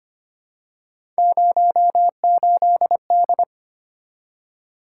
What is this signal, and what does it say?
Morse code, heard as a keyed tone, about 25 wpm, 700 Hz. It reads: 08D
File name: Three Letter-Number Combinations - 25wpm